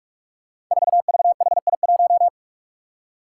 Morse code, heard as a keyed tone, about 44 wpm, 700 Hz. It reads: 4VHI1
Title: Five Challenging Letter-Number Combinations 45wpm